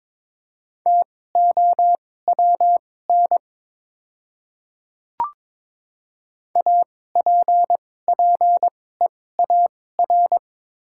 Morse code, heard as a keyed tone, about 22 wpm, spaced as for 16 wpm, 700 Hz. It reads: TOWN APPEAR
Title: 2x Character Spacing - Top 500 Words - 22wpm